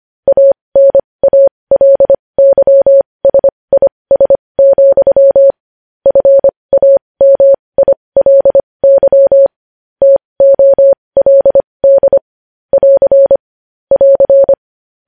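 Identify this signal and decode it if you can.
Morse code, heard as a keyed tone, about 25 wpm, spaced as for 20 wpm, 550 Hz. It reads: ANALYSIS, FAMILY TOLD <AR> <AR>
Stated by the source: Morse code news headlines at 20 wpm.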